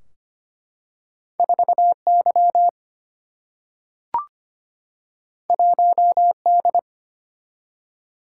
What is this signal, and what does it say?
Morse code, heard as a keyed tone, about 25 wpm, 700 Hz. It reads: 4Y 1D
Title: Single-Letter Single-Number 25wpm